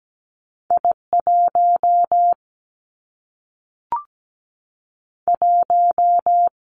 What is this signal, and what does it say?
Morse code, heard as a keyed tone, about 17 wpm, 700 Hz. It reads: I1 1
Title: Two Challenging Letter-Number Combinations 17wpm